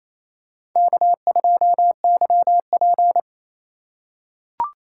Morse code, heard as a keyed tone, about 28 wpm, 700 Hz. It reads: K2YP
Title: USA Call Signs - 4 Characters - 2x Character Spacing - 28wpm